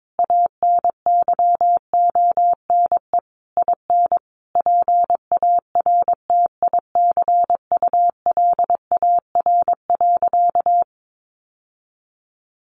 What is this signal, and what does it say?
Morse code, heard as a keyed tone, about 22 wpm, 700 Hz. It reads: ANYONE IN PARTICULAR.
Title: Sentences from Top 900 Words 22wpm